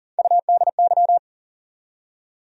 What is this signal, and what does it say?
Morse code, heard as a keyed tone, about 39 wpm, 700 Hz. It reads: UDY
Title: USA Call Sign Postfixes - 3 Letters 40wpm